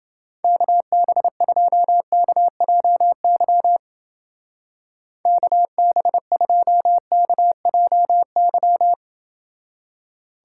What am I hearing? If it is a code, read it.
Morse code, heard as a keyed tone, about 28 wpm, 700 Hz. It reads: KB2KJY KB2KJY